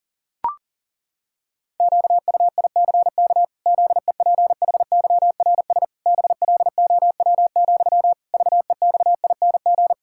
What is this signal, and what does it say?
Morse code, heard as a keyed tone, about 40 wpm, 700 Hz. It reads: QUICK ZEPHYRS BLOW, VEXING
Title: Pangram Challenge v2 - 40wpm